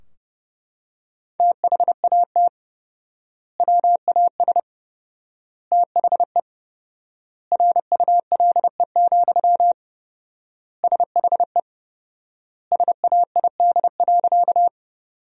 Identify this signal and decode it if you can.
Morse code, heard as a keyed tone, about 30 wpm, 700 Hz. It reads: THAT WAS THE RULE, SHE SAID.
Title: 4x Word-Spacing - Sentences from Top 400 Words - 30wpm